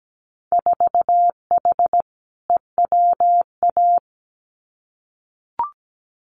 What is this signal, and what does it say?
Morse code, heard as a keyed tone, about 17 wpm, 700 Hz. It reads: 4H EWA
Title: Winter Field Day Exchanges Encore 17wpm